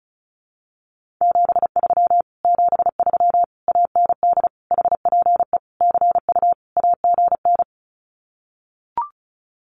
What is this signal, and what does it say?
Morse code, heard as a keyed tone, about 34 wpm, 700 Hz. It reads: 73 73 AND HPE CU AGN
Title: Common QSO Phrases Encore 35wpm